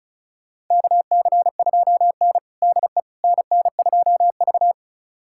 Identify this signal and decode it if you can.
Morse code, heard as a keyed tone, about 36 wpm, 700 Hz. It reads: KC2N DE NN2V
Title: USA Callsign DE Callsign - 4 Characters - 35wpm